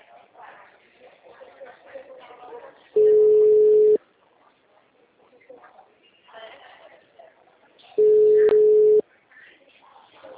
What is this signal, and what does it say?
Ringback tone (Europe, 425 Hz), with speech.